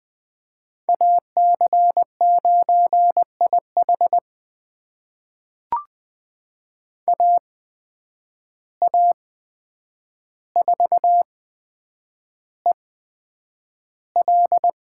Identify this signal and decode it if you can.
Morse code, heard as a keyed tone, about 20 wpm, spaced as for 9 wpm, 700 Hz. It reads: AC9IH AA4EL
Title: USA Call Signs - 5 Characters - 8x Character Spacing - 20wpm